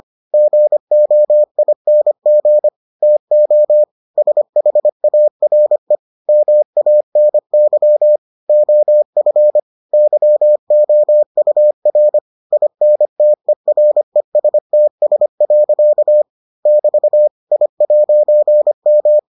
Morse code, heard as a keyed tone, about 25 wpm, 600 Hz. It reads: GOING TO SHARE MANY OF YOUR INTERESTS. <BT> I'M